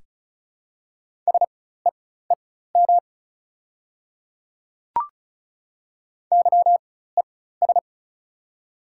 Morse code, heard as a keyed tone, about 35 wpm, spaced as for 16 wpm, 690 Hz. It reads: SEEM YES
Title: 4x Character Spacing - Top 500 Words - 35wpm